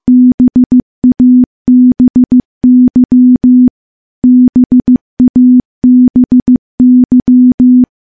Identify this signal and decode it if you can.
Morse code, heard as a keyed tone, about 15 wpm, 250 Hz. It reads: BABY BABY
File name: morse / morse-training-15WPM